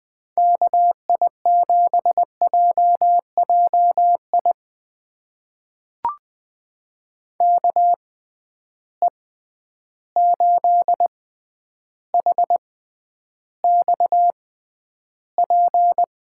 Morse code, heard as a keyed tone, about 20 wpm, spaced as for 10 wpm, 700 Hz. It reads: KI7JJI KE8HXP
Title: USA Call Signs - 6 Characters - 6x Character Spacing - 20wpm